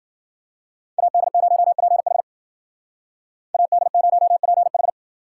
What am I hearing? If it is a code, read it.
Morse code, heard as a keyed tone, about 48 wpm, 700 Hz. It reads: AD0PH AD0PH
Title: Speed Racing - USA 2x2 Call Signs 40wpm